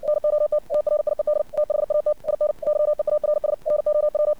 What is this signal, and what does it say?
Morse code, heard as a keyed tone, about 57 wpm, 600 Hz. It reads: NOT TRIED THAT IN YEARS NOW
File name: archived / CRSnet